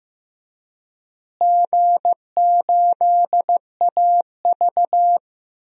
Morse code, heard as a keyed tone, about 16 wpm, 700 Hz. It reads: G8AV
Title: DXCC-100 Call Signs - 4 Characters - Repeated 2x - 15wpm